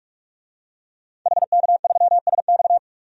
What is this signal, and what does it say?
Morse code, heard as a keyed tone, about 46 wpm, 700 Hz. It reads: HK3SX
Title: DXCC-100 Call Signs - 5 Characters - 45wpm